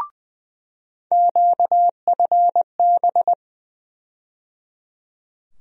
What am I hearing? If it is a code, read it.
Morse code, heard as a keyed tone, about 21 wpm, 700 Hz. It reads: QFB
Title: USA Call Sign Postfixes - 3 Letters 20wpm